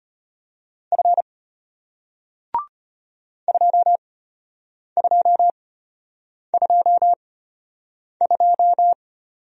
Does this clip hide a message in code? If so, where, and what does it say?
Yes: Morse code from 0.9 to 8.9 s, heard as a keyed tone, about 33 wpm, spaced as for 8 wpm, 700 Hz. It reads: F 2222